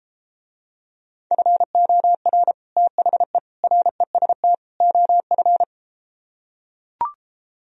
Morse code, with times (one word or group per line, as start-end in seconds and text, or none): 1.3-2.5 FOR
2.8-3.4 THE
3.6-4.5 REST
4.8-5.6 OF